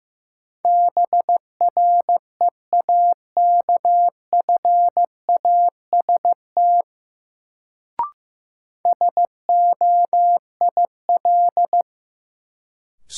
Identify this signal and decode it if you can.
Morse code, heard as a keyed tone, about 16 wpm, 700 Hz. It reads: BREAKFAST SOIL
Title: Sentences from Top 1500 Words - New Word Drill Alt - 15wpm